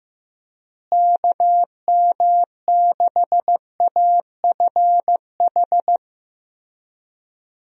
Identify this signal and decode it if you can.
Morse code, heard as a keyed tone, about 15 wpm, 700 Hz. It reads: KM6AFH
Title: USA Call Signs - 6 Characters - Repeated 2x - 15wpm